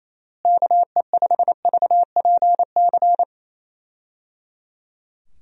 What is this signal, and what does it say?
Morse code, heard as a keyed tone, about 27 wpm, 700 Hz. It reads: KE5VPC
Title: USA Call Signs - 6 Characters - Repeated 3x - 28wpm